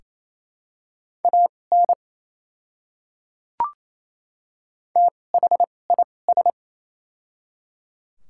Morse code, heard as a keyed tone, about 28 wpm, spaced as for 20 wpm, 700 Hz. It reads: AN THIS
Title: 2x Character Spacing - Top 100 Words - 28wpm